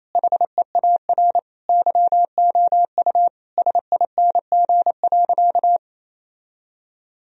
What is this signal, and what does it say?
Morse code, heard as a keyed tone, about 28 wpm, 700 Hz. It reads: HEAR YOU SING.
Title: Sentences from Top 1000 Words - Review Part 2 - 28wpm